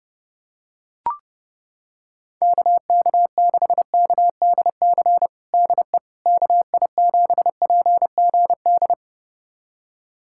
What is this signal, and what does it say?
Morse code, heard as a keyed tone, about 32 wpm, 700 Hz. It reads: KK6KDC DE KI7PGD
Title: USA Callsign DE Callsign 30wpm